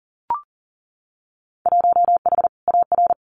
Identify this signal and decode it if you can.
Morse code, heard as a keyed tone, about 41 wpm, 700 Hz. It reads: 1H AR
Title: Winter Field Day Exchanges 40wpm